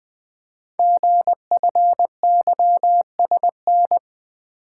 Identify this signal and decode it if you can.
Morse code, heard as a keyed tone, about 20 wpm, 700 Hz. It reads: GFYSN
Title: Code Groups - 5 Letter:Numbers - 20wpm